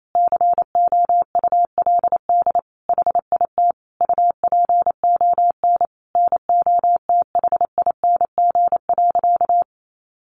Morse code, heard as a keyed tone, about 28 wpm, 700 Hz. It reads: COULD HIT UPON NOTHING.
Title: Sentences from Top 1200 Words 28wpm